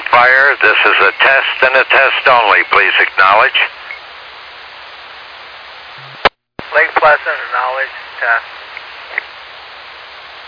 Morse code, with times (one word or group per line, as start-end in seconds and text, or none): none